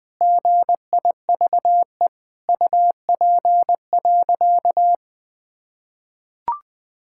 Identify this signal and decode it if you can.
Morse code, heard as a keyed tone, about 21 wpm, 700 Hz. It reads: GIVE UP.
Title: Sentences with 5-Word Phrases - 20wpm